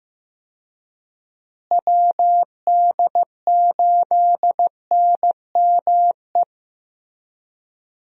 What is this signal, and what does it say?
Morse code, heard as a keyed tone, about 15 wpm, 700 Hz. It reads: WD8NME
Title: USA Call Signs - 6 Characters - Repeated 2x - 15wpm